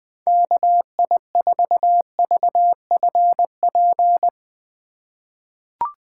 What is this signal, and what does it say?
Morse code, heard as a keyed tone, about 20 wpm, 700 Hz. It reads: KI4VFP
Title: USA Call Signs - 6 Characters - 4x Character Spacing - 20wpm